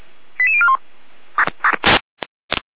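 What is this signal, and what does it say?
Five-tone selective call (ZVEI), heard as quick runs of five tones, tones of about 70 ms. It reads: 88041
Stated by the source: ZVEI five-tone selective call received on SDR.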